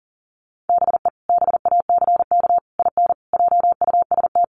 Morse code, heard as a keyed tone, about 40 wpm, 700 Hz. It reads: BE BACK IN JUST
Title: Sets of 5 Words Encore 40wpm